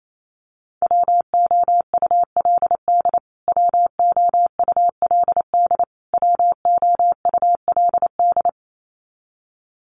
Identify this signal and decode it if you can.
Morse code, heard as a keyed tone, about 28 wpm, 700 Hz. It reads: WOULD WOULD WOULD
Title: Top 500 Words Repeated 28wpm